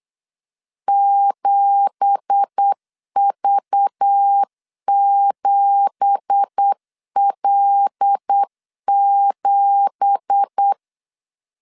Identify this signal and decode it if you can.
Morse code, heard as a keyed tone, about 8 wpm, 800 Hz. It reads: 7V7L7
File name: 8wpm